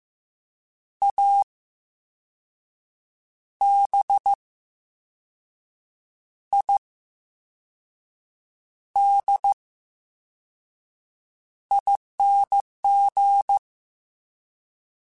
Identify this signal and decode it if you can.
Morse code, heard as a keyed tone, about 15 wpm, spaced as for 5 wpm, 800 Hz. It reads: ABIDING
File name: Special INGs / Special INGs-mp3s / Slow ing @13